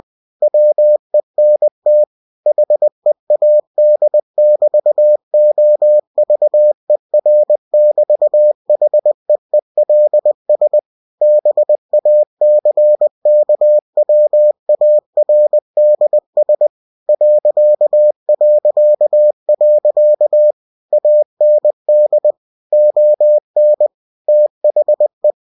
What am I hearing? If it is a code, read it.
Morse code, heard as a keyed tone, about 20 wpm, 600 Hz. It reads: WENT HEAD<BT>OVER<BT>HEELS BACKWARDS ... AND ON THE
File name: cw-books / winnie-the-pooh / winnie-20wpm-page-0020